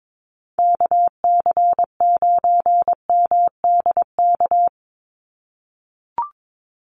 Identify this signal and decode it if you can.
Morse code, heard as a keyed tone, about 22 wpm, 700 Hz. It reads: KC9MDK
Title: USA Call Signs 22wpm